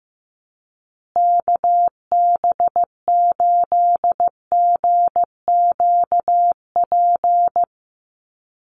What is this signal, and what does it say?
Morse code, heard as a keyed tone, about 15 wpm, 700 Hz. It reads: KB8GQP
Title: USA Call Signs Repeated 3x 15wpm